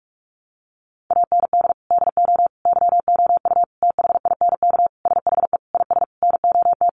Morse code, heard as a keyed tone, about 44 wpm, 700 Hz. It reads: AND DO YOU THINK SHE IS NOT
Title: Sentences from Top 600 Words 45wpm